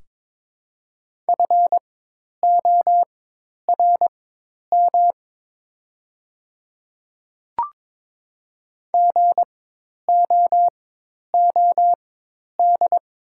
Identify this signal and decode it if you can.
Morse code, heard as a keyed tone, about 22 wpm, spaced as for 10 wpm, 700 Hz. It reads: FORM GOOD
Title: 4x Character Spacing - Top 200 Words - 22wpm